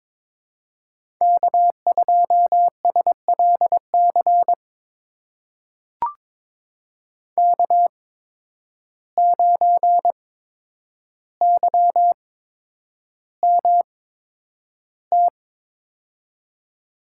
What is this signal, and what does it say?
Morse code, heard as a keyed tone, about 22 wpm, spaced as for 9 wpm, 700 Hz. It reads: K2SLC K9YMT
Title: USA Call Signs - 5 Characters - 8x Character Spacing - 22wpm